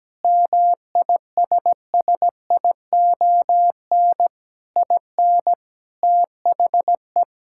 Morse code, heard as a keyed tone, about 17 wpm, 700 Hz. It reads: MISSION IN THE
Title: Sentences from Top 1600 Words 17wpm